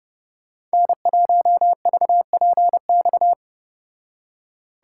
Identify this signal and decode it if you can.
Morse code, heard as a keyed tone, about 33 wpm, 700 Hz. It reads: N1VPX